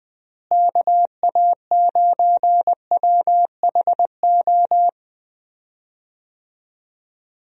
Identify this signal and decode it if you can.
Morse code, heard as a keyed tone, about 20 wpm, 700 Hz. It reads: KA9WHO
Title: USA Call Signs - 6 Characters - Repeated 2x - 20wpm